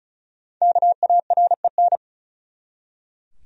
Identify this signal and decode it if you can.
Morse code, heard as a keyed tone, about 36 wpm, 700 Hz. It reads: KAREN
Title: USA Common Names - 35wpm